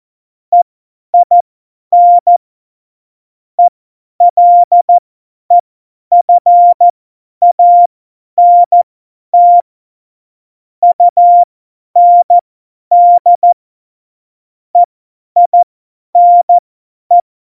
Morse code, heard as a keyed tone, about 14 wpm, spaced as for 10 wpm, 700 Hz. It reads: EIN ELEFANT UND EINE
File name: Morsekurs / Witze / Witz 11. Dezember, 50 F70